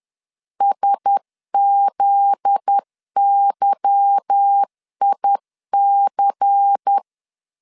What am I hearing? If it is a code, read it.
Morse code, heard as a keyed tone, about 10 wpm, 800 Hz. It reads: SZYIC
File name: LEZIONI / 9 286 10wpm 2735